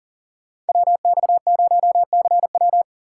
Morse code, heard as a keyed tone, about 52 wpm, 700 Hz. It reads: WX0CW